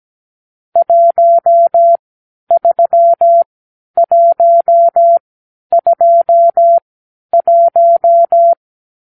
Morse code, heard as a keyed tone, about 17 wpm, spaced as for 10 wpm, 680 Hz. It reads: 13121
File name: k7qo / Lesson 44 (1-3)